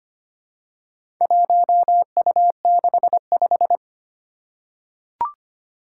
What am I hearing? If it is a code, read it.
Morse code, heard as a keyed tone, about 25 wpm, 700 Hz. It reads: 1U65
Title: Four Challenging Letter-Number Combinations 25wpm